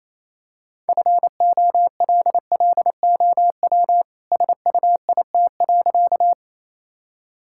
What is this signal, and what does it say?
Morse code, heard as a keyed tone, about 28 wpm, 700 Hz. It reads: FOLLOW SUIT.
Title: Sentences from Top 1600 Words 28wpm